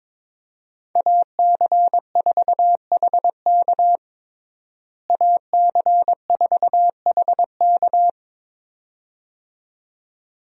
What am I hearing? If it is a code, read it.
Morse code, heard as a keyed tone, about 21 wpm, 700 Hz. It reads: AC4HK AC4HK
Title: USA Call Signs - 5 Characters - Repeated 2x - 22wpm